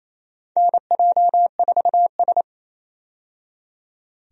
Morse code, heard as a keyed tone, about 27 wpm, 700 Hz. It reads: NJ4S
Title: USA Call Signs - 4 Characters - 28wpm